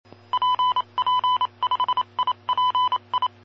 Morse code, heard as a keyed tone, about 28 wpm, 1000 Hz. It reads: PP5IPI
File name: aulas de telegrafia